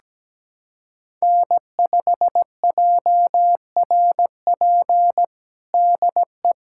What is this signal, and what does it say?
Morse code, heard as a keyed tone, about 17 wpm, 700 Hz. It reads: N5JRP DE
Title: USA Callsign DE Callsign - 5 Characters - 17wpm